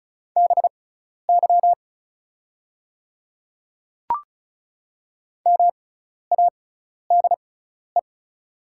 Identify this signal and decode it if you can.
Morse code, heard as a keyed tone, about 36 wpm, spaced as for 12 wpm, 700 Hz. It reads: BY MADE
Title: 6x Character Spacing - Top 300 Words - 35wpm